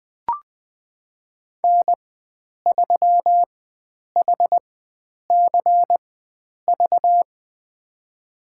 Morse code, heard as a keyed tone, about 20 wpm, spaced as for 9 wpm, 700 Hz. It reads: N3HCV
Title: USA Call Signs - 5 Characters - 4x Character Spacing - 20wpm